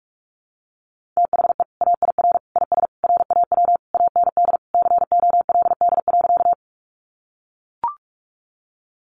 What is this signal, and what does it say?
Morse code, heard as a keyed tone, about 44 wpm, 700 Hz. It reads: THE AIR IS RAW AND COLD.
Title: Sentences from Top 1500 Words 45wpm